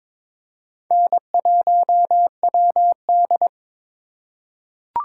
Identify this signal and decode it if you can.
Morse code, heard as a keyed tone, about 23 wpm, 700 Hz. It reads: N1WD
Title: USA Call Signs - 4 Characters - 3x Character Spacing - 22wpm